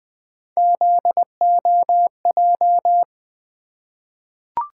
Morse code, heard as a keyed tone, about 20 wpm, 700 Hz. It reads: ZOJ
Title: Three Letter-Number Combinations - 20wpm